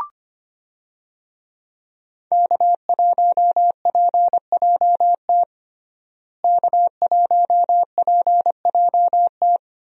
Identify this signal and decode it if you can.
Morse code, heard as a keyed tone, about 25 wpm, 700 Hz. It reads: K1PJT K1PJT
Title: USA Call Signs - 5 Characters - Repeated 3x - 25wpm